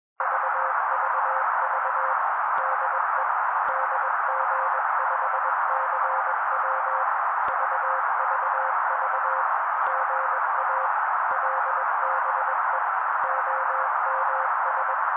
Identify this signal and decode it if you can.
Morse code, heard as a keyed tone, about 21 wpm, 530 Hz. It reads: VVV DE DG5CW VVV GA LBE OMS
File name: daily / 20wpm noise